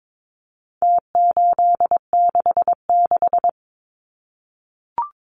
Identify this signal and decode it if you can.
Morse code, heard as a keyed tone, about 22 wpm, 700 Hz. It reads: T866
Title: Numbers - Contest Style - 4-Digit 22wpm